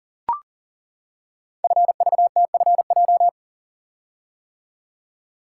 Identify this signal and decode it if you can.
Morse code, heard as a keyed tone, about 40 wpm, 700 Hz. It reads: FVTFJ
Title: Code Groups - 5 Letters - 40wpm